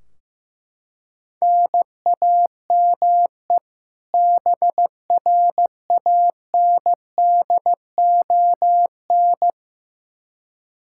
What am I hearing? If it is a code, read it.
Morse code, heard as a keyed tone, about 15 wpm, 700 Hz. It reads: NAME BRANDON
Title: K1USN SST QSO Practice 15wpm